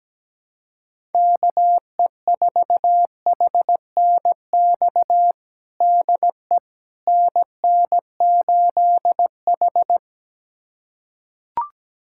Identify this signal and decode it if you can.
Morse code, heard as a keyed tone, about 17 wpm, 700 Hz. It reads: KE4HNX DE NN8H
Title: USA Callsign DE Callsign 17wpm